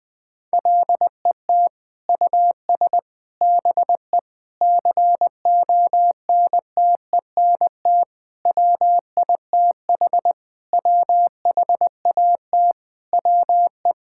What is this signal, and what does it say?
Morse code, heard as a keyed tone, about 20 wpm, 700 Hz. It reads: LET US BE CONTENT WITH WHAT WE
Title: Sentences from Top 1000 Words - Review Part 6 - 20wpm